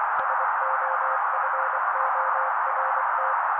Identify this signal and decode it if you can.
Morse code, heard as a keyed tone, about 24 wpm, 530 Hz. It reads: SOFORT
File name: daily / 23wpm noise